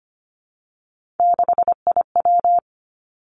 Morse code, heard as a keyed tone, about 25 wpm, 700 Hz. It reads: 6IW
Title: Three Challenging Letter-Number Combinations 25wpm